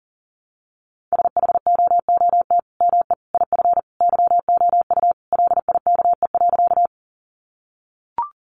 Morse code, heard as a keyed tone, about 40 wpm, 700 Hz. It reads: SHOOT ME IF YOU LIKE.
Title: Sentences from Top 1200 Words 40wpm